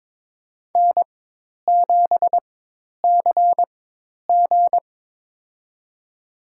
Morse code, heard as a keyed tone, about 23 wpm, spaced as for 10 wpm, 700 Hz. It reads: N7CG
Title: USA Call Signs - 4 Characters - 4x Character Spacing - 22wpm